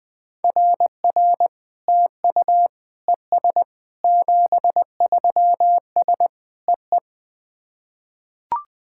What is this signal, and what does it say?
Morse code, heard as a keyed tone, about 20 wpm, 700 Hz. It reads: RR TU ES 73S EE
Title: SOTA QSO Practice 20wpm